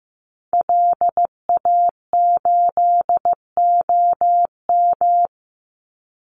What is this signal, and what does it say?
Morse code, heard as a keyed tone, about 15 wpm, 700 Hz. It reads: LA8OM
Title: Call Signs Repeated 3x - 15wpm